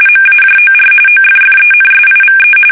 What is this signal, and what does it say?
RTTY, about 45.45 baud. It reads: MAJOR OFFENSIVE
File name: fsk850 / battleshipnj